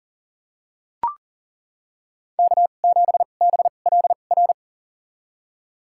Morse code, heard as a keyed tone, about 40 wpm, spaced as for 29 wpm, 700 Hz. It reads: K7BLR
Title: USA Call Signs - 5 Characters - 2x Character Spacing - 40wpm